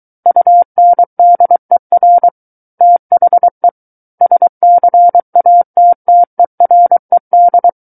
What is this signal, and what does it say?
Morse code, heard as a keyed tone, about 23 wpm, 700 Hz. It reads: UNDER THE SCATTERED
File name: Book One / 14-Chapter 14 - 23 WPM